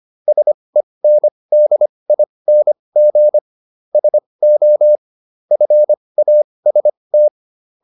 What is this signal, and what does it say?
Morse code, heard as a keyed tone, about 25 wpm, spaced as for 20 wpm, 600 Hz. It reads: SENDING SO FAST